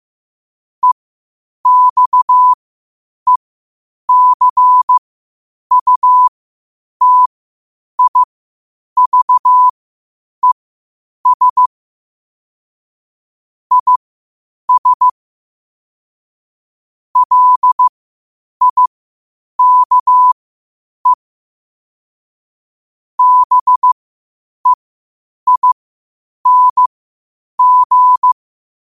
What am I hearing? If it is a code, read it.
Morse code, heard as a keyed tone, about 15 wpm, spaced as for 8 wpm, 1000 Hz. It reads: EXECUTIVES IS LIKE BEING
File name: qotdfiles / 8wpm